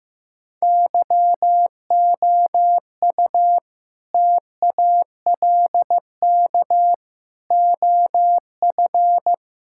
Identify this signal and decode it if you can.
Morse code, heard as a keyed tone, about 15 wpm, 700 Hz. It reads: YOU TALK OF
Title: Sentences from Top 1600 Words 15wpm